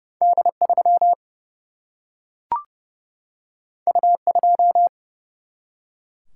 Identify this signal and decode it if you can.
Morse code, heard as a keyed tone, about 30 wpm, 700 Hz. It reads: D3 U2
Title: Single-Letter Single-Number 30wpm